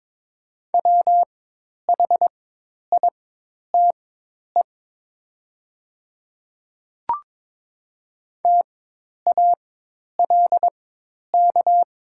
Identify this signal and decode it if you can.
Morse code, heard as a keyed tone, about 22 wpm, spaced as for 10 wpm, 700 Hz. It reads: WHITE TALK
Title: 4x Character Spacing - Top 500 Words - 22wpm